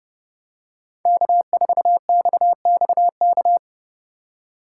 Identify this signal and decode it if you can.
Morse code, heard as a keyed tone, about 29 wpm, 700 Hz. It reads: K4XXK